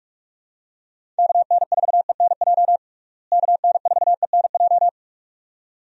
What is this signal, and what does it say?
Morse code, heard as a keyed tone, about 46 wpm, 700 Hz. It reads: KN4ENJ KN4ENJ
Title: USA Call Signs - 6 Characters - Repeated 2x - 45wpm